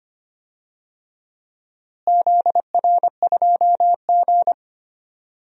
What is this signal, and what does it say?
Morse code, heard as a keyed tone, about 25 wpm, 700 Hz. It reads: ZR2G
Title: DXCC-100 Call Signs - 4 Characters - Repeated 3x - 25wpm